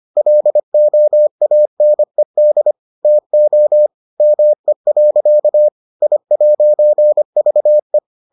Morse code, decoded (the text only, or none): LOANED TO ME. I'VE